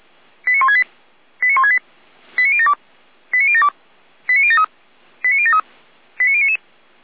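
Five-tone selective call (ZVEI), heard as quick runs of five tones, tones of about 70 ms. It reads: 89168 89168 89072 89072 89073 89073 89099
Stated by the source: Received by SDR; ZVEI five-tone code 89150.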